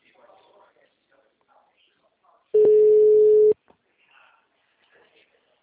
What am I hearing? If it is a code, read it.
Ringback tone (Europe, 425 Hz), with speech.
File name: out- -2016- - - .503